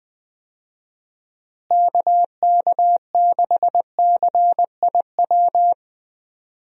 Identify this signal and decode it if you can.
Morse code, heard as a keyed tone, about 20 wpm, 700 Hz. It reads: KK6CIW
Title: USA Call Signs - 6 Characters - Repeated 2x - 20wpm